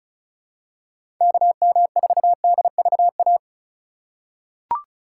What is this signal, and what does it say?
Morse code, heard as a keyed tone, about 36 wpm, 700 Hz. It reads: KM4DVA
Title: USA Call Signs - 6 Characters - 4x Character Spacing - 35wpm